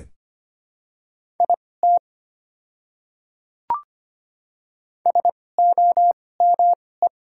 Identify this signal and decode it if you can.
Morse code, heard as a keyed tone, about 25 wpm, spaced as for 18 wpm, 700 Hz. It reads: IT SOME
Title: 2x Character Spacing - Top 100 Words - 25wpm